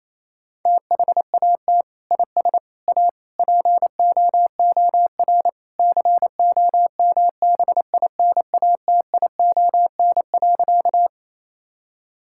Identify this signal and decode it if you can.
Morse code, heard as a keyed tone, about 28 wpm, 700 Hz. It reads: THAT IS A POOR COMBINATION.
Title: Sentences from Top 1000 Words - Review Part 6 - 28wpm